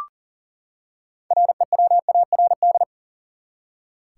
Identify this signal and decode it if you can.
Morse code, heard as a keyed tone, about 40 wpm, 700 Hz. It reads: REWARD
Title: Six Letter Words - 40wpm